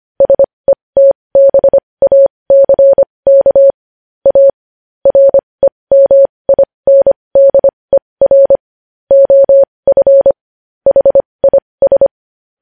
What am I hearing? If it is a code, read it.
Morse code, heard as a keyed tone, about 25 wpm, spaced as for 20 wpm, 550 Hz. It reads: SETBACK A REMINDER OF HIS
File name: podcast / News headlines-1x-25x20